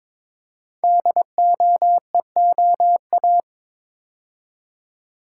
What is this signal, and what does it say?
Morse code, heard as a keyed tone, about 22 wpm, 700 Hz. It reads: DOEOA
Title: Code Groups - 5 Letters - 22wpm